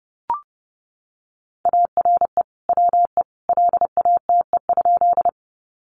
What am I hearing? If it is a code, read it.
Morse code, heard as a keyed tone, about 30 wpm, 700 Hz. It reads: ARE WE LATE?
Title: Sentences from Top 300 Words 30wpm